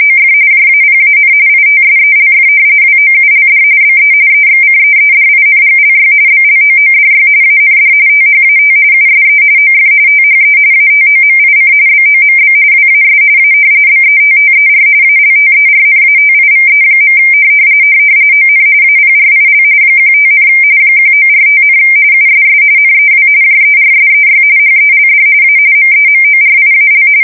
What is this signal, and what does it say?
RTTY, about 45.45 baud. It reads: OF JAPAN ITSELF WOULD BE EXCEEDINGLY BLOODY AND THAT USE OF THE ATOMIC BOMB WAS A BETTER ALTERNATIVE. ON APRIL 15, 1945, USS NEW JERSEY HELD A MEMORIAL
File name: fsk170 / battleshipnj